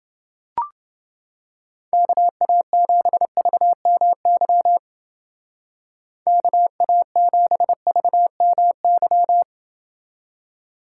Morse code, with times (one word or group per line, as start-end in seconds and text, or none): 1.9-4.8 KA7VMY
6.3-9.4 KA7VMY